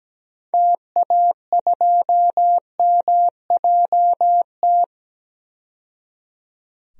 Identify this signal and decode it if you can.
Morse code, heard as a keyed tone, about 17 wpm, 700 Hz. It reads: TA2MJT
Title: DXCC-100 Call Signs - 6 Characters - 17wpm